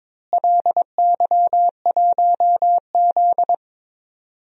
Morse code, heard as a keyed tone, about 22 wpm, 700 Hz. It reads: LY1Z
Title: DXCC-100 Call Signs - 4 Characters - Repeated 3x - 22wpm